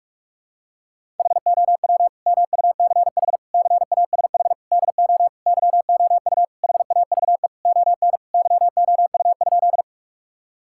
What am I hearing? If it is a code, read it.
Morse code, heard as a keyed tone, about 46 wpm, 700 Hz. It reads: HOW MUCH CASH DO YOU HAVE ON YOU?
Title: Sentences from Top 1000 Words - Review Part 5 - 45wpm